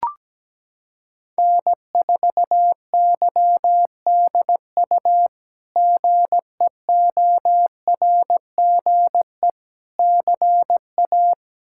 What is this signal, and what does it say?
Morse code, heard as a keyed tone, about 17 wpm, 700 Hz. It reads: N4YDU GEORGE CA
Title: K1USN SST QSO Practice - 17wpm